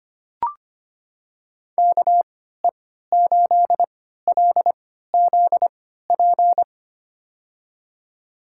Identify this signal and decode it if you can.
Morse code, heard as a keyed tone, about 25 wpm, spaced as for 14 wpm, 700 Hz. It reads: KE8LZP
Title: USA Call Signs - 6 Characters - 3x Character Spacing - 25wpm